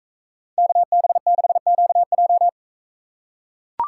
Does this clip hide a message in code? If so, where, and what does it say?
Yes: Morse code from 0.6 to 2.5 s, heard as a keyed tone, about 44 wpm, 700 Hz. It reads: KB6QJ